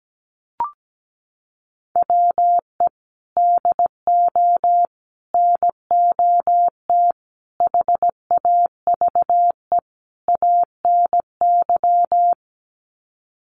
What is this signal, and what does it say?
Morse code, heard as a keyed tone, about 17 wpm, 700 Hz. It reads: WE DO NOT HAVE ANY
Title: Sets of 5 Words 17wpm